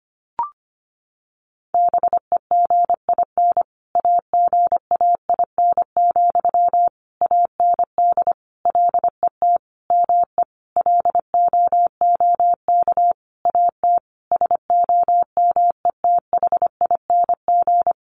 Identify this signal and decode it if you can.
Morse code, heard as a keyed tone, about 25 wpm, 700 Hz. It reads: BEGIN AGAIN, AND LET ME LOOK AT SOMETHING
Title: Sentences from Top 800 Words 25wpm